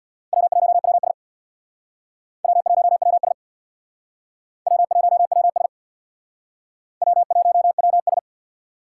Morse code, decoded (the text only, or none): W1WS W1WS W1WS W1WS